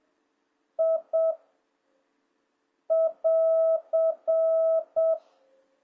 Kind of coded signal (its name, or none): Morse code